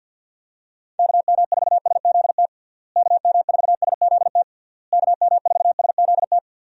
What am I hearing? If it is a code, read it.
Morse code, heard as a keyed tone, about 48 wpm, 700 Hz. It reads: KM4SZT KM4SZT KM4SZT
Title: USA Call Signs - 6 Characters - Repeated 3x - 50wpm